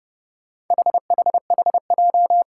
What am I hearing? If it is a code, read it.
Morse code, heard as a keyed tone, about 28 wpm, 700 Hz. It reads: HHHJ
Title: Four Challenging Letter-Number Combinations 30wpm